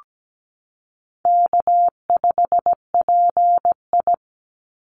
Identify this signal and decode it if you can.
Morse code, heard as a keyed tone, about 17 wpm, 700 Hz. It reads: K5PI
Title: Easy Call Signs Repeated 3x - 17wpm